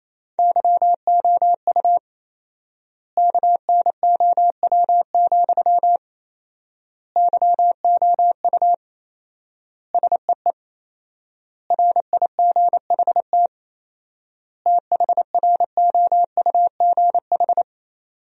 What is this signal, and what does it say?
Morse code, heard as a keyed tone, about 28 wpm, 700 Hz. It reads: YOU KNOW, YOU SEE RIGHT THROUGH
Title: 4x Word-Spacing - Sentences from Top 200 Words - 28wpm